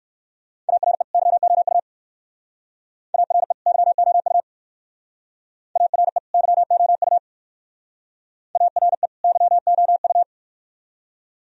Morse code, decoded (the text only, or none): ARE YOU ARE YOU ARE YOU ARE YOU